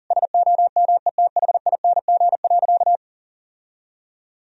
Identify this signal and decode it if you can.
Morse code, heard as a keyed tone, about 40 wpm, 700 Hz. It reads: SOMETHING.
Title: Sentences from Top 1000 Words - Review Part 1 - 40wpm